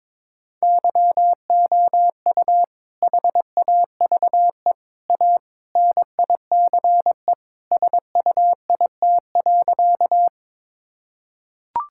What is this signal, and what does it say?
Morse code, heard as a keyed tone, about 22 wpm, 700 Hz. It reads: YOU HAVE A NICE SUIT.
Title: Sentences from Top 1600 Words 22wpm